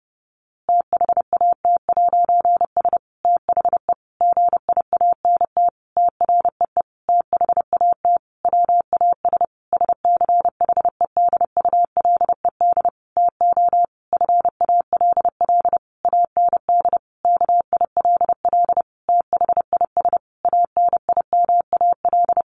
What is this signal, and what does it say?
Morse code, heard as a keyed tone, about 30 wpm, 700 Hz. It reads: THAT'S THE GIANT TREE THAT WAS SCHEDULED TO FALL AND KILL THIS ANIMAL
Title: A Sound Of Thunder 30wpm